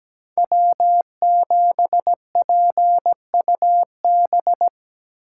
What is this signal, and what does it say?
Morse code, heard as a keyed tone, about 20 wpm, 700 Hz. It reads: W7PUB
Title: Speed Racing - USA 1x3 Call Signs 17wpm